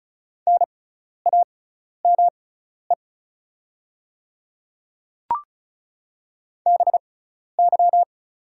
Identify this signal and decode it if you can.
Morse code, heard as a keyed tone, about 36 wpm, spaced as for 12 wpm, 700 Hz. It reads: NAME BY
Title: 6x Character Spacing - Top 200 Words - 35wpm